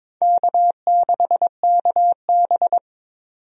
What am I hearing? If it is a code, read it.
Morse code, heard as a keyed tone, about 22 wpm, 700 Hz. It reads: K6KB
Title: USA Call Signs - 4 Characters - 22wpm